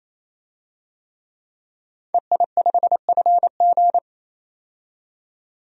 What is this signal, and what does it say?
Morse code, heard as a keyed tone, about 27 wpm, 690 Hz. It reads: EI5FG
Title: DXCC-100 Call Signs - 5 Characters - 28wpm